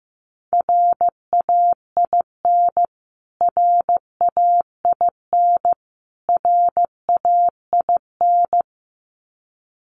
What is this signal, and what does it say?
Morse code, heard as a keyed tone, about 15 wpm, 700 Hz. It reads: RAIN RAIN RAIN
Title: Top 300 Words Repeated 15wpm